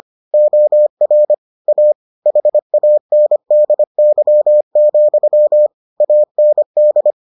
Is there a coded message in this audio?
Yes: Morse code from 0.3 to 7.1 s, heard as a keyed tone, about 25 wpm, 600 Hz. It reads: OR A HANDY, AND